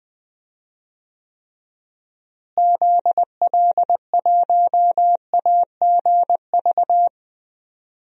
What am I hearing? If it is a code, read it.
Morse code, heard as a keyed tone, about 20 wpm, 700 Hz. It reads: ZL1AGV
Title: DXCC-100 Call Signs - 6 Characters - Repeated 3x - 20wpm